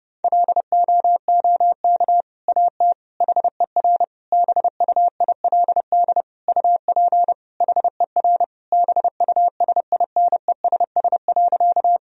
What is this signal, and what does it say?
Morse code, heard as a keyed tone, about 32 wpm, 700 Hz. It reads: LOOK AT HER BUILD UP HER BUSINESS.
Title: Sentences from Top 1000 Words - Review Part 1 - 30wpm